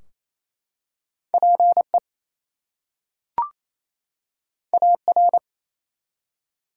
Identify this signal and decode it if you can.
Morse code, heard as a keyed tone, about 29 wpm, 700 Hz. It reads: PE AR
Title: US States and Canadian Provinces 28wpm